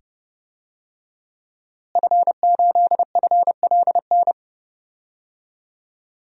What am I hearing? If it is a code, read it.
Morse code, heard as a keyed tone, about 31 wpm, 700 Hz. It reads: F8FLN
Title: DXCC-100 Call Signs - 5 Characters - Repeated 2x - 30wpm